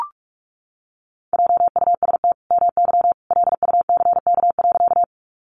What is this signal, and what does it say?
Morse code, heard as a keyed tone, about 44 wpm, 700 Hz. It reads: JUST MY LUCK.
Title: Sentences from Top 1500 Words 45wpm